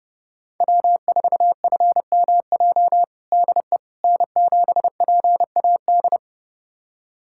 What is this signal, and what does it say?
Morse code, heard as a keyed tone, about 32 wpm, 700 Hz. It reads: W4FMJ DE N7PAD
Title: USA Callsign DE Callsign - 5 Characters - 30wpm